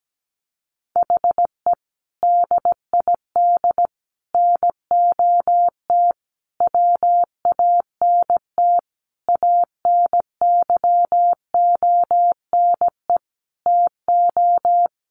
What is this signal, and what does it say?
Morse code, heard as a keyed tone, about 17 wpm, 700 Hz. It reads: HE DID NOT WANT ANYONE TO
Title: Sentences from Top 1100 Words 17wpm